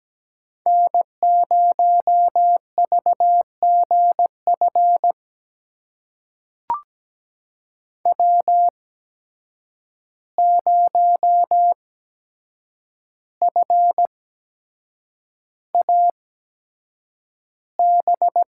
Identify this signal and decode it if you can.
Morse code, heard as a keyed tone, about 17 wpm, spaced as for 7 wpm, 700 Hz. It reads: N0VGF W0FAB
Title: USA Call Signs - 5 Characters - 8x Character Spacing - 17wpm